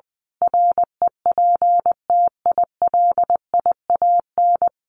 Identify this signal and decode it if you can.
Morse code, heard as a keyed tone, about 20 wpm, 700 Hz. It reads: REPTILIAN
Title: A Sound Of Thunder 20wpm